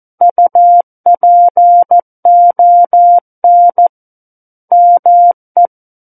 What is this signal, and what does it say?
Morse code, heard as a keyed tone, about 14 wpm, 700 Hz. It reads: UPON ME
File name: Book One / 05-Chapter 5 - 14 WPM